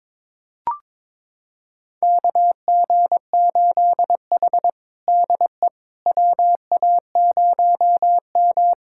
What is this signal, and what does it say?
Morse code, heard as a keyed tone, about 22 wpm, 700 Hz. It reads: KG8H DE WA0M
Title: USA Callsign DE Callsign - 4 Characters - 22wpm